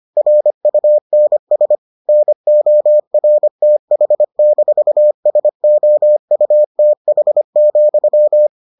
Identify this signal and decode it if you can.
Morse code, heard as a keyed tone, about 25 wpm, 600 Hz. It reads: RUNS NORTH-SOUTH,